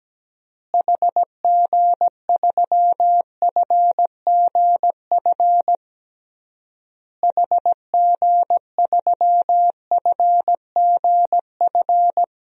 Morse code, heard as a keyed tone, about 17 wpm, 700 Hz. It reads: HG3FGF HG3FGF
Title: DXCC-100 Call Signs - 6 Characters - Repeated 3x - 17wpm